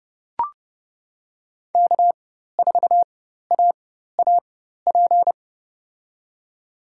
Morse code, heard as a keyed tone, about 30 wpm, spaced as for 14 wpm, 700 Hz. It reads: K4AAP
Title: USA Call Signs - 5 Characters - 4x Character Spacing - 30wpm